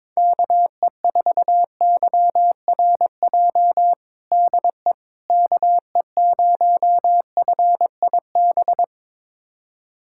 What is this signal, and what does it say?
Morse code, heard as a keyed tone, about 22 wpm, 700 Hz. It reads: KE4YRJ DE KE0FIB
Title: USA Callsign DE Callsign - 6 Characters - 22wpm